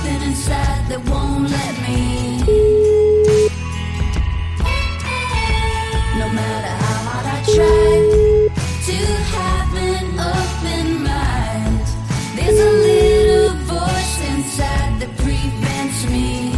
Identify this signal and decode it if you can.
Ringback tone (Europe, 425 Hz), with noise.